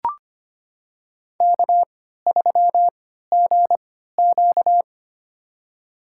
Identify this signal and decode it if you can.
Morse code, heard as a keyed tone, about 25 wpm, spaced as for 14 wpm, 700 Hz. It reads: K3GQ
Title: USA Call Signs - 4 Characters - 3x Character Spacing - 25wpm